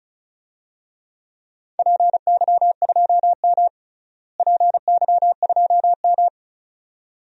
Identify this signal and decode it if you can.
Morse code, heard as a keyed tone, about 36 wpm, 700 Hz. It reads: PY2M PY2M
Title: DXCC-100 Call Signs - 4 Characters - Repeated 2x - 35wpm